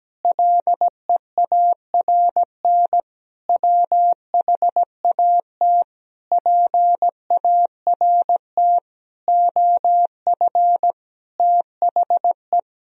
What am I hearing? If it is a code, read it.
Morse code, heard as a keyed tone, about 17 wpm, 700 Hz. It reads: LEARN WHAT PART OF THE
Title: Sentences from Top 1000 Words - Review Part 3 - 17wpm